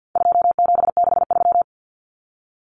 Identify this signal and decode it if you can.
Morse code, heard as a keyed tone, about 50 wpm, 700 Hz. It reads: 2763
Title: Common Cut Numbers - 4-Digit 50wpm